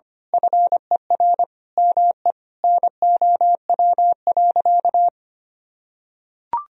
Morse code, heard as a keyed tone, about 25 wpm, 700 Hz. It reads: FER ME NOW.
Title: Sentences from Top 200 Words and Abbreviations 25wpm